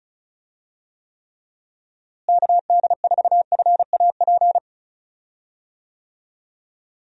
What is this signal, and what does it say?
Morse code, heard as a keyed tone, about 36 wpm, 700 Hz. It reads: KD4FAP
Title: USA Call Signs - 6 Characters - Repeated 2x - 35wpm